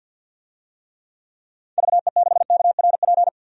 Morse code, heard as a keyed tone, about 47 wpm, 700 Hz. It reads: VE6KRP
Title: DXCC-100 Call Signs - 6 Characters - 50wpm